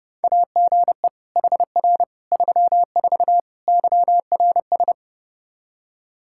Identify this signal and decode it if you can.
Morse code, heard as a keyed tone, about 30 wpm, 700 Hz. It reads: AGE HR 34 YRS
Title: Practice QSO - Exchange 3 - More Info - 30wpm